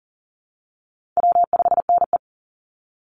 Morse code, heard as a keyed tone, about 40 wpm, 700 Hz. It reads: W5NE
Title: Easy Call Signs Repeated 2x - 40wpm